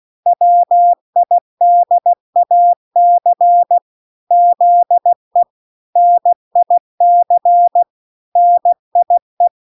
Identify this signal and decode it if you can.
Morse code, heard as a keyed tone, about 16 wpm, 700 Hz. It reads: WIDAC ZE NIC NIE